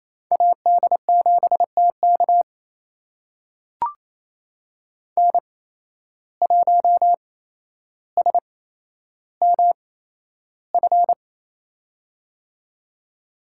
Morse code, heard as a keyed tone, about 28 wpm, spaced as for 12 wpm, 700 Hz. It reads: AD7TK N1SMF
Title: USA Call Signs - 5 Characters - 8x Character Spacing - 28wpm